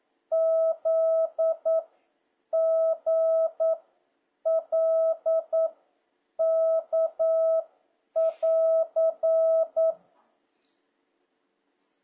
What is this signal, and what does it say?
Morse code, heard as a keyed tone, about 9 wpm, 650 Hz. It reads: Z G L K <AR>